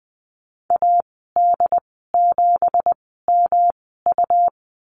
Morse code, heard as a keyed tone, about 20 wpm, spaced as for 15 wpm, 700 Hz. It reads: AD7MU
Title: 193 Lesson 31g - Letter v - Call Signs 20wpm